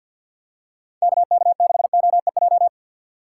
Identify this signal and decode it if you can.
Morse code, heard as a keyed tone, about 48 wpm, 700 Hz. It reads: KK6OEJ
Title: USA Call Signs - 3x Character Spacing - 50wpm